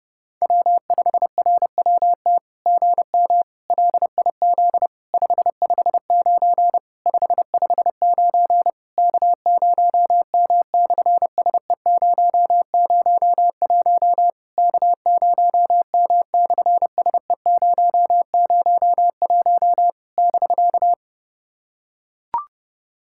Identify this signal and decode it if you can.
Morse code, heard as a keyed tone, about 26 wpm, 700 Hz. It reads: W5RWT GM LIZ 559 559 K0M/SE001 K0M/SE001 <BK>
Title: SOTA QSO Practice 30wpm